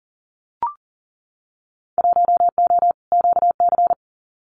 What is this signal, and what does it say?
Morse code, heard as a keyed tone, about 38 wpm, 700 Hz. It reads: 1O QC
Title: Winter Field Day Exchanges 40wpm